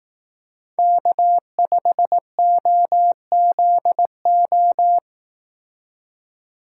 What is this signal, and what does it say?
Morse code, heard as a keyed tone, about 18 wpm, 700 Hz. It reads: K5OZO
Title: Speed Racing - USA 1x3 Call Signs 15wpm